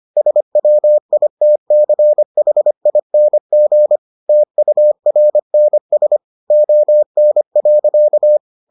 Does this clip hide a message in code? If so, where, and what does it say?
Yes: Morse code from 0.2 to 8.4 s, heard as a keyed tone, about 25 wpm, 600 Hz. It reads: SWITCHING TURNS ON.